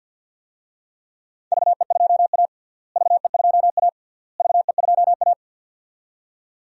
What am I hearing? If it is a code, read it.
Morse code, heard as a keyed tone, about 48 wpm, 700 Hz. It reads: VE2A VE2A VE2A
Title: DXCC-100 Call Signs - 4 Characters - Repeated 3x - 50wpm